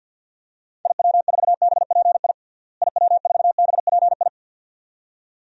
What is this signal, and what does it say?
Morse code, heard as a keyed tone, about 50 wpm, 700 Hz. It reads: IW4BPI IW4BPI
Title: DXCC-100 Call Signs - 6 Characters - Repeated 2x - 50wpm